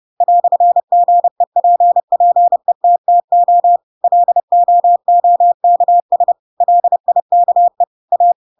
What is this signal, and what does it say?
Morse code, heard as a keyed tone, about 30 wpm, 700 Hz. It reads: "GEPPETTO LOOKS LIKE A
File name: Pinocchio / Pinocchio ch3 30 30 ik1hge